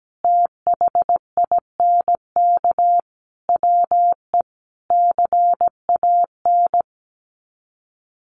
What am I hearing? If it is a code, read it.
Morse code, heard as a keyed tone, about 17 wpm, 700 Hz. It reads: THINK WE CAN
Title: Sets of 5 Words Encore2 17wpm